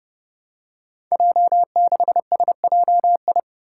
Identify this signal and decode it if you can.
Morse code, heard as a keyed tone, about 30 wpm, 700 Hz. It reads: J6SJI
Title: Five Challenging Letter-Number Combinations 30wpm